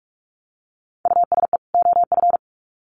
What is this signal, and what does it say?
Morse code, heard as a keyed tone, about 44 wpm, 700 Hz. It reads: USE OF
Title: Sets of 2 Words 45wpm